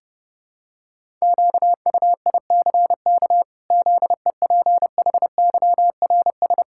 Morse code, heard as a keyed tone, about 28 wpm, 700 Hz. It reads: QUICK ZEPHYRS
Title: Pangram Challenge v2 - 30wpm